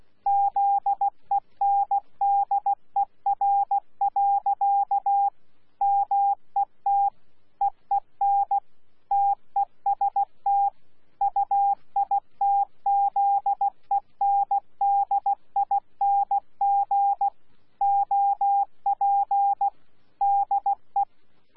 Morse code, heard as a keyed tone, about 16 wpm, 800 Hz. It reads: ZENDER. MET EEN TEST UITZENDING OP DE
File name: Z PI4VRZ / PI4VRZ 14-jun-2025